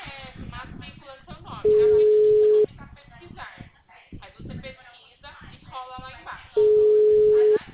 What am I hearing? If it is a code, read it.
Ringback tone (Europe, 425 Hz), with speech.